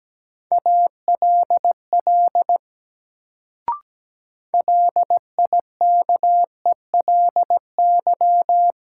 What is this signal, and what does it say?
Morse code, heard as a keyed tone, about 17 wpm, 700 Hz. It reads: ALL LIKELY
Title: Sets of 2 Words - New Word Drill - 17wpm